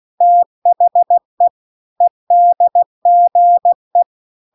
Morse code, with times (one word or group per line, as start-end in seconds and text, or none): 0.2-1.5 THE
2.0-4.0 EDGE